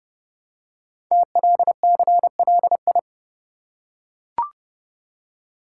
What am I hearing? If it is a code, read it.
Morse code, heard as a keyed tone, about 30 wpm, 700 Hz. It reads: TLCLI
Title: Code Groups - 5 Letter:Numbers - 30wpm